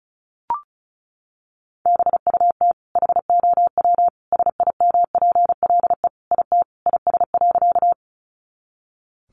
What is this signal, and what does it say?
Morse code, heard as a keyed tone, about 34 wpm, 700 Hz. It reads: BUT HOW SIMPLE IT IS.
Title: Sentences from Top 400 Words 35wpm